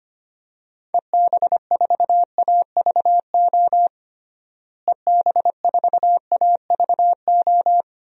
Morse code, heard as a keyed tone, about 25 wpm, 690 Hz. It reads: EB4AVO EB4AVO
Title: DXCC-100 Call Signs - 6 Characters - Repeated 3x - 25wpm